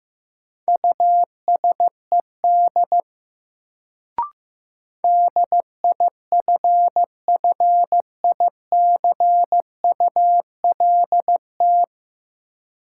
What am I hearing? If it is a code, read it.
Morse code, heard as a keyed tone, about 15 wpm, 700 Hz. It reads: USED DIFFICULT
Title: Sentences from Top 600 Words - New Word Drill - 15wpm